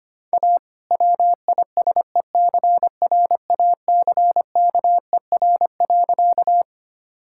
Morse code, heard as a keyed tone, about 25 wpm, 700 Hz. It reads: A WISECRACKER.
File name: practice / 25wpm / All-Age Appropriate Jokes - 25wpm